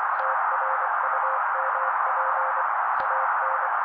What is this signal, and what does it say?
Morse code, heard as a keyed tone, about 24 wpm, 530 Hz. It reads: TRUMP AN
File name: daily / 23wpm noise